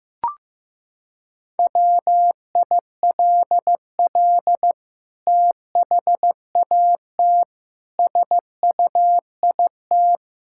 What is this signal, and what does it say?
Morse code, heard as a keyed tone, about 14 wpm, 700 Hz. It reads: WILL THAT SUIT
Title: Sentences from Top 1600 Words 15wpm